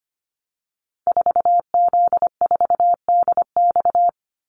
Morse code, heard as a keyed tone, about 25 wpm, 700 Hz. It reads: 4Z4DX
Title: Call Signs - 25wpm